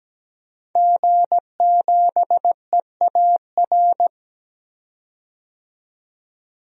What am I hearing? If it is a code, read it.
Morse code, heard as a keyed tone, about 17 wpm, 700 Hz. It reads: G7EAR
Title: DXCC-100 Call Signs - 5 Characters - 17wpm